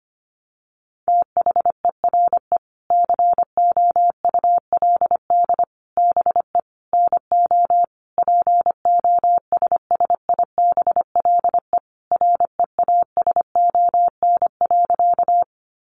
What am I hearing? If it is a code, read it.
Morse code, heard as a keyed tone, about 25 wpm, 700 Hz. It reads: THERE COULD BE NO POSSIBLE REASON.
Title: Sentences from Top 700 Words 25wpm